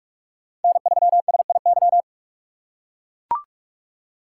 Morse code, heard as a keyed tone, about 43 wpm, 700 Hz. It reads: N3SIY